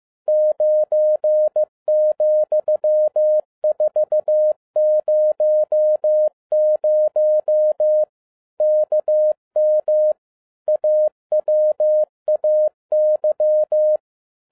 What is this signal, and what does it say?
Morse code, heard as a keyed tone, about 15 wpm, 610 Hz. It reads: 9,400 KM AWAY
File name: 2011-10-mp3 / n12-JL3YMV 15wpm